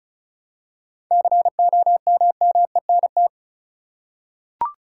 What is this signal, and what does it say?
Morse code, heard as a keyed tone, about 36 wpm, 700 Hz. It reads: COMMENT
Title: Seven Letter Words - 35wpm